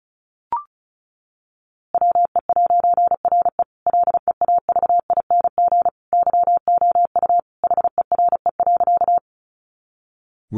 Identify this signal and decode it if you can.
Morse code, heard as a keyed tone, about 34 wpm, 700 Hz. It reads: WE'RE LEAVING YOU HERE.
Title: A Sound Of Thunder 35wpm